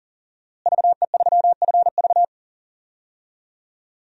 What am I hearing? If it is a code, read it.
Morse code, heard as a keyed tone, about 40 wpm, 700 Hz. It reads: VE3FV
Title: DXCC-100 Call Signs - 5 Characters - Repeated 3x - 40wpm